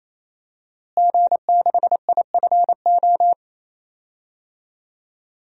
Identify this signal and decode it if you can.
Morse code, heard as a keyed tone, about 27 wpm, 700 Hz. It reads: G6IFO
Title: DXCC-100 Call Signs - 5 Characters - Repeated 3x - 28wpm